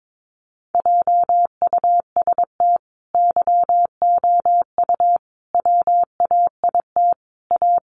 Morse code, heard as a keyed tone, about 22 wpm, 700 Hz. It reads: JUST YOU WAIT A
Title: Sentences from Top 900 Words 22wpm